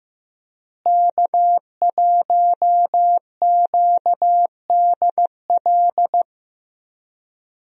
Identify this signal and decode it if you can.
Morse code, heard as a keyed tone, about 15 wpm, 700 Hz. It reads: K1QDL
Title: USA Call Signs - 5 Characters - Repeated 3x - 15wpm